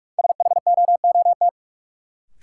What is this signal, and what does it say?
Morse code, heard as a keyed tone, about 46 wpm, 700 Hz. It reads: SHOOT
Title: Sentences from Top 1200 Words - New Word Drill - 45wpm